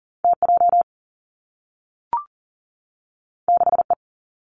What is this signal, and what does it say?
Morse code, heard as a keyed tone, about 40 wpm, 700 Hz. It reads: T J 6 E
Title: Two Challenging Letter-Number Combinations 40wpm